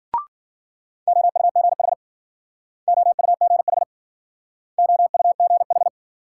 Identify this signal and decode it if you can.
Morse code, heard as a keyed tone, about 50 wpm, 700 Hz. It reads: OUGH OUGH OUGH
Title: Speed-Racing Multi-letter Phonemes - 40wpm